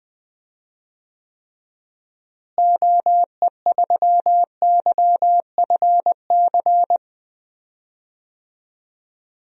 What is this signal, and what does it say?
Morse code, heard as a keyed tone, about 20 wpm, 700 Hz. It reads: OE3YFC
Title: DXCC-100 Call Signs - 6 Characters - Repeated 2x - 20wpm